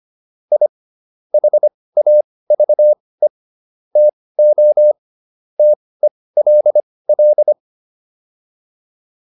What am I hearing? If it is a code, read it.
Morse code, heard as a keyed tone, about 25 wpm, spaced as for 18 wpm, 600 Hz. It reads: I HAVE TO TELL
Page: intermediate-practice-files